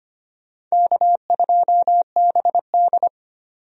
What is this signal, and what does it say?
Morse code, heard as a keyed tone, about 25 wpm, 700 Hz. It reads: K2BD
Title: USA Call Signs - 4 Characters - 25wpm